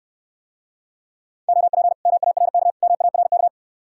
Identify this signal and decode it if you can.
Morse code, heard as a keyed tone, about 69 wpm, 700 Hz. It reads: OP MARK MARK